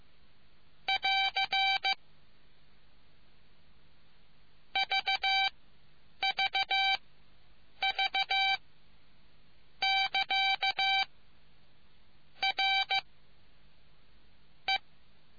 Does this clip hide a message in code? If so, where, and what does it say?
Yes: Morse code from 0.9 to 14.8 s, heard as a keyed tone, about 15 wpm, spaced as for 7 wpm, 780 Hz. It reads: <AR> VVV<KA> R E